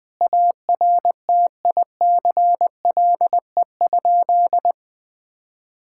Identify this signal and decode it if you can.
Morse code, heard as a keyed tone, about 21 wpm, 700 Hz. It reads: ARTICLE?
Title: Sentences from Top 1000 Words - Review Part 5 - 20wpm